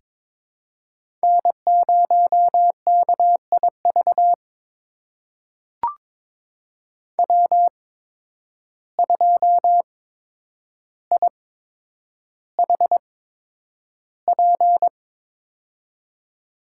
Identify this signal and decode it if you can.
Morse code, heard as a keyed tone, about 22 wpm, spaced as for 9 wpm, 700 Hz. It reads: N0KIV W2IHP
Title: USA Call Signs - 5 Characters - 8x Character Spacing - 22wpm